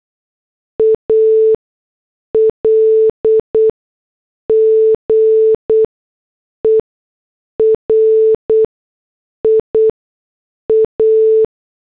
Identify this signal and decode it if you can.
Morse code, heard as a keyed tone, about 8 wpm, spaced as for 6 wpm, 440 Hz. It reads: ALGERIA